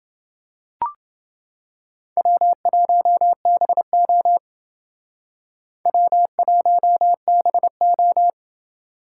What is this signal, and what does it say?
Morse code, heard as a keyed tone, about 29 wpm, 700 Hz. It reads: W1BO W1BO